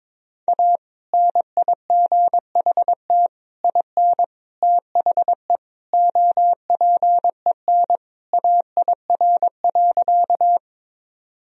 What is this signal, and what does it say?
Morse code, heard as a keyed tone, about 22 wpm, 700 Hz. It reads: A NIGHT IN THE OPEN AIR.
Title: Sentences from Top 1000 Words - Review Part 1 - 22wpm